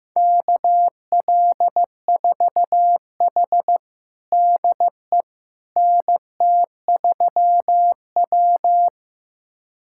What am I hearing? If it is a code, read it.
Morse code, heard as a keyed tone, about 16 wpm, 700 Hz. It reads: KL4H DE NT3W
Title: USA Callsign DE Callsign - 4 Characters - 15wpm